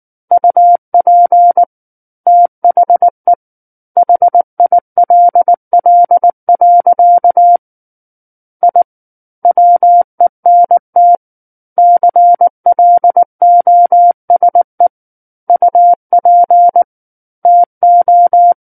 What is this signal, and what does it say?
Morse code, heard as a keyed tone, about 19 wpm, 700 Hz. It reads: UP THE HILL. I WENT CLOSE UP TO
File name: Book One / 10-Chapter 10 - 19 WPM